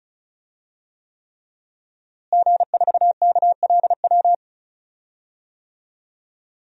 Morse code, heard as a keyed tone, about 36 wpm, 700 Hz. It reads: G4KLW
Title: DXCC-100 Call Signs - 5 Characters - Repeated 3x - 35wpm